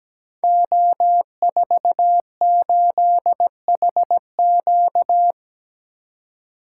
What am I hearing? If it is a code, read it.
Morse code, heard as a keyed tone, about 17 wpm, 700 Hz. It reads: O48HQ
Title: Code Groups - 5 Letter:Numbers - 17wpm